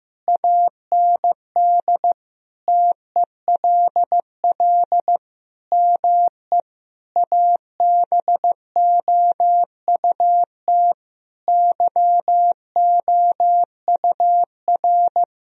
Morse code, heard as a keyed tone, about 15 wpm, 700 Hz. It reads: AND TELL ME ABOUT YOUR
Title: Sentences from Top 1000 Words - Review Part 3 - 15wpm